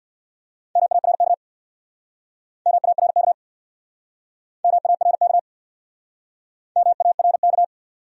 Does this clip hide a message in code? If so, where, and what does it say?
Yes: Morse code from 0.8 to 7.6 s, heard as a keyed tone, about 64 wpm, 700 Hz. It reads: MARK MARK MARK MARK